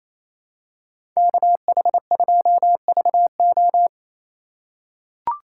Morse code, heard as a keyed tone, about 28 wpm, 700 Hz. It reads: KH2VO
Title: USA Call Signs - 4x Character Spacing - 28wpm